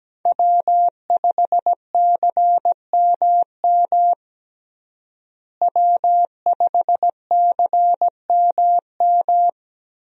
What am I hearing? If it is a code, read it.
Morse code, heard as a keyed tone, about 17 wpm, 700 Hz. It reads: W5CMM W5CMM
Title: USA Call Signs - 5 Characters - Repeated 2x - 17wpm